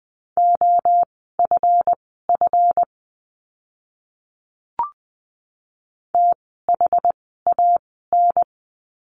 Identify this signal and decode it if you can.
Morse code, heard as a keyed tone, about 20 wpm, spaced as for 15 wpm, 700 Hz. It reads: OFF THAN
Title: 244 Lesson 39f - Letter z - Words 20wpm